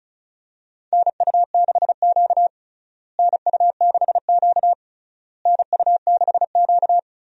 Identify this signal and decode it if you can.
Morse code, heard as a keyed tone, about 34 wpm, 700 Hz. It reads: NU6Q NU6Q NU6Q
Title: USA Call Signs - 4 Characters - Repeated 3x - 35wpm